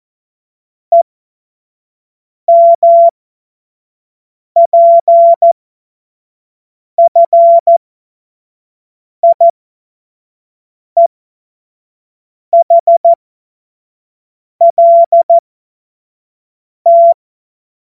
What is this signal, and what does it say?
Morse code, heard as a keyed tone, about 14 wpm, spaced as for 5 wpm, 680 Hz. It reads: EMPFIEHLT
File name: Morsekurs / Witze / Witz 27. Januar 2026 25 F70